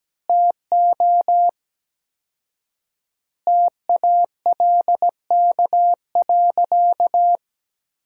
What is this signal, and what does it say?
Morse code, heard as a keyed tone, about 17 wpm, 700 Hz. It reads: TO TALK.
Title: 4x Word-Spacing - Sentences from Top 400 Words - 17wpm